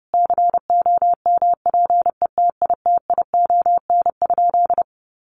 Morse code, heard as a keyed tone, about 30 wpm, 700 Hz. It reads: COMPETITION?
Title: Sentences from Top 1000 Words 30wpm